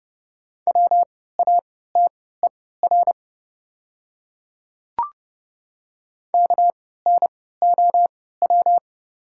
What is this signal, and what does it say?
Morse code, heard as a keyed tone, about 30 wpm, spaced as for 17 wpm, 690 Hz. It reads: WATER KNOW
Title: 3x Character Spacing - Top 200 Words - 30wpm